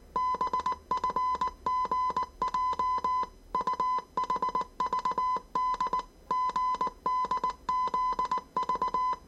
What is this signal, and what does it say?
Morse code, heard as a keyed tone, about 19 wpm, 1010 Hz. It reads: BFGJ UHVD GDZV
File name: 业余无线电 / ●未分类 / 字码80